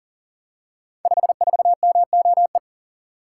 Morse code, heard as a keyed tone, about 42 wpm, 700 Hz. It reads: 54MOE
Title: Code Groups - 5 Letter:Numbers - 40wpm